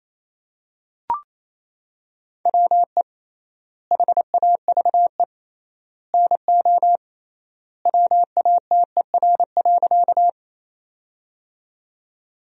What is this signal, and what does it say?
Morse code, heard as a keyed tone, about 28 wpm, 700 Hz. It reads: WE HAVE NO WATER.
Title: 3x Word-Spacing - Sentences from Top 100 Words - 28wpm